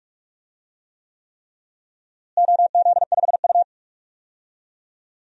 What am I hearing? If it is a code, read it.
Morse code, heard as a keyed tone, about 46 wpm, 700 Hz. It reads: OZ5U
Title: DXCC-100 Call Signs - 4 Characters - Repeated 3x - 45wpm